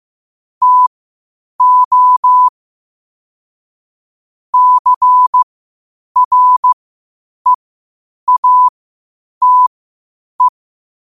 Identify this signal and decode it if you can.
Morse code, heard as a keyed tone, about 15 wpm, spaced as for 8 wpm, 1000 Hz. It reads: TO CREATE
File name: qotdfiles / 8wpm